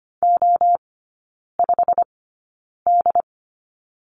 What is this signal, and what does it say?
Morse code, heard as a keyed tone, about 25 wpm, spaced as for 9 wpm, 690 Hz. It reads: O5D
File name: Lesson 15e - Number 5 - Three Characters 25wpm